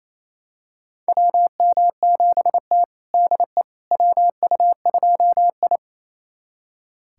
Morse code, heard as a keyed tone, about 28 wpm, 700 Hz. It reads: WM7T DE WU2I
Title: USA Callsign DE Callsign - 4 Characters - 28wpm